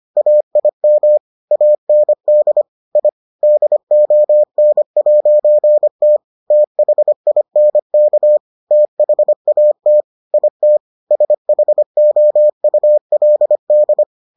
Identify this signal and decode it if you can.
Morse code, heard as a keyed tone, about 25 wpm, 600 Hz. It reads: AIM AND I DON'T THINK THAT IT SHOULD